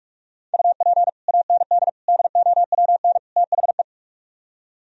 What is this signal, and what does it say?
Morse code, heard as a keyed tone, about 45 wpm, 700 Hz. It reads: UP AND DOWN THE